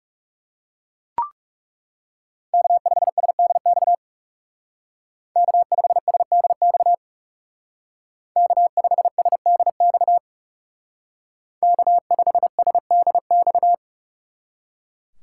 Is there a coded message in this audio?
Yes: Morse code from 2.5 to 13.7 s, heard as a keyed tone, about 39 wpm, 700 Hz. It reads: K5SDX K5SDX K5SDX K5SDX